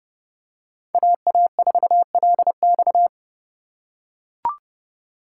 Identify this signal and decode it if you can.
Morse code, heard as a keyed tone, about 29 wpm, 700 Hz. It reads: AA4LX